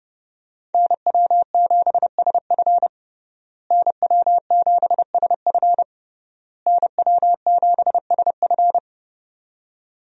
Morse code, heard as a keyed tone, about 30 wpm, 700 Hz. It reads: NW7SF NW7SF NW7SF
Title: USA Call Signs - 5 Characters - Repeated 3x - 30wpm